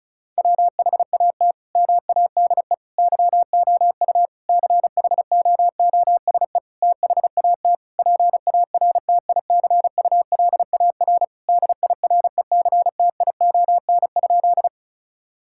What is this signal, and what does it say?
Morse code, heard as a keyed tone, about 36 wpm, 700 Hz. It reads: WHAT MADE YOU CHOOSE THAT PARTICULAR DIRECTION?
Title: Sentences from Top 1000 Words - Review Part 5 - 35wpm